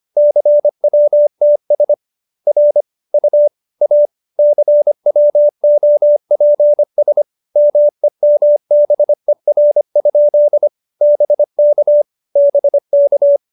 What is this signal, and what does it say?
Morse code, heard as a keyed tone, about 25 wpm, 590 Hz. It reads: CWTS R U A CWOPS MEMBER? BK BK